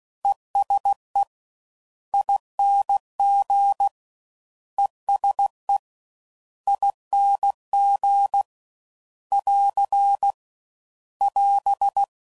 Morse code, heard as a keyed tone, about 16 wpm, 800 Hz. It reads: ESE ING ESE ING <AR> <AS>
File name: Suffixes / General / ESE INGs